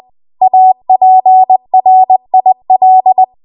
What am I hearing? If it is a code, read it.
Morse code, heard as a keyed tone, about 20 wpm, 750 Hz. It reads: APRIL
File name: Morse / 20WPM